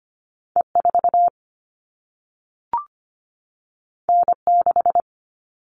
Morse code, heard as a keyed tone, about 25 wpm, 700 Hz. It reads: E4 N6
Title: All Cut Numbers - 2-Digit 25wpm